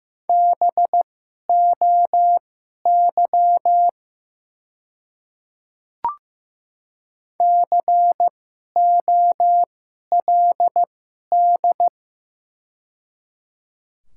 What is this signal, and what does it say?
Morse code, heard as a keyed tone, about 15 wpm, spaced as for 11 wpm, 700 Hz. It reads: BOY COLD
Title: 2x Character Spacing - Top 500 Words - 15wpm